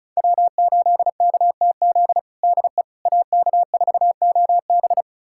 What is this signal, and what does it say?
Morse code, heard as a keyed tone, about 36 wpm, 700 Hz. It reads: W8KTZ DE AK4OB
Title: USA Callsign DE Callsign - 5 Characters - 35wpm